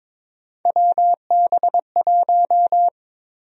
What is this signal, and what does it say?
Morse code, heard as a keyed tone, about 22 wpm, 700 Hz. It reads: WB1
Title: USA Call Sign Prefixes - 22wpm